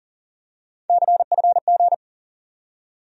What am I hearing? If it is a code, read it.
Morse code, heard as a keyed tone, about 40 wpm, 700 Hz. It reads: CFG
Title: USA Call Sign Postfixes - 3 Letters 40wpm